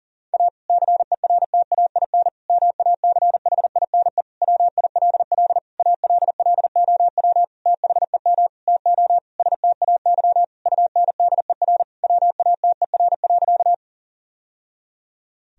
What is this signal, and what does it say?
Morse code, heard as a keyed tone, about 40 wpm, 700 Hz. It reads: A CERTAIN MACHINE WILL ALLOW THEM TO STAY UNDER WATER.
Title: Sentences from Top 1000 Words - Review Part 2 - 40wpm